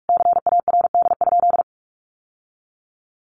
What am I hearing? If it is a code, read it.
Morse code, heard as a keyed tone, about 44 wpm, 700 Hz. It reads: CARD?
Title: Sentences from Top 1100 Words 45wpm